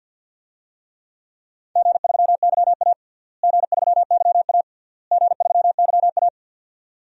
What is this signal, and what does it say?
Morse code, heard as a keyed tone, about 50 wpm, 700 Hz. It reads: G3YA G3YA G3YA
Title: DXCC-100 Call Signs - 4 Characters - Repeated 3x - 50wpm